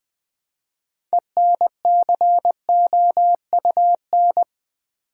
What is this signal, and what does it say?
Morse code, heard as a keyed tone, about 20 wpm, 700 Hz. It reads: ENCOUN
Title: Build A Word - 3 Letters At A Time - 9-Letter Words - 20wpm